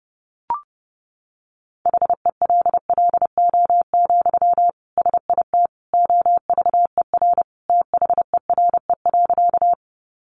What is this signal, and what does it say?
Morse code, heard as a keyed tone, about 30 wpm, 700 Hz. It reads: HELLO, SIT OVER THERE.
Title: Sentences from Top 1100 Words 30wpm